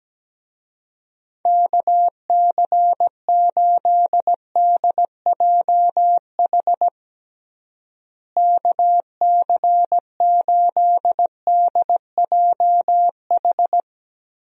Morse code, heard as a keyed tone, about 17 wpm, 700 Hz. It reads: KC8DJH KC8DJH
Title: USA Call Signs - 6 Characters - Repeated 3x - 17wpm